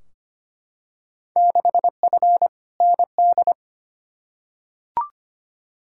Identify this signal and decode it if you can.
Morse code, heard as a keyed tone, about 25 wpm, 700 Hz. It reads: 6F ND
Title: Field Day Exchanges Encore - 25wpm